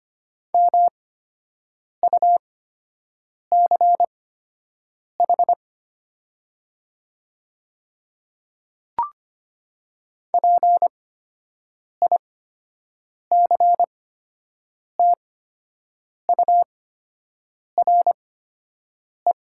Morse code, heard as a keyed tone, about 25 wpm, spaced as for 7 wpm, 700 Hz. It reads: MUCH PICTURE
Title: 8x Character Spacing - Top 200 Words - 25wpm